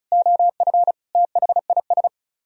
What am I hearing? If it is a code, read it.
Morse code, heard as a keyed tone, about 45 wpm, 700 Hz. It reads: OF THIS